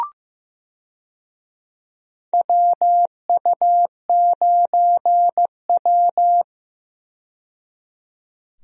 Morse code, heard as a keyed tone, about 15 wpm, 700 Hz. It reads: WU9W
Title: USA Call Signs - 4 Characters - 15wpm